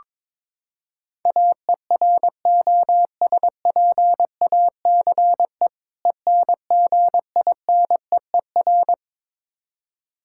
Morse code, heard as a keyed tone, about 21 wpm, 700 Hz. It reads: AEROSPACE ENGINEER
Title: Practice QSO - Exchange 3 - More Info - 22wpm